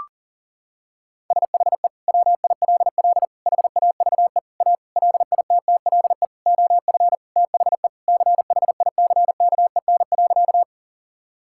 Morse code, heard as a keyed tone, about 40 wpm, 700 Hz. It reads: SHE WILL HAVE A LITTLE OF THE CHICKEN.
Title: Sentences from Top 1000 Words - Review Part 5 - 40wpm